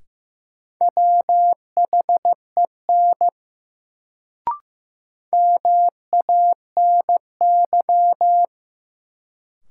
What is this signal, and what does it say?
Morse code, heard as a keyed tone, about 15 wpm, 700 Hz. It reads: WHEN MANY
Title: Sets of 2 Words - New Word Drill - 15wpm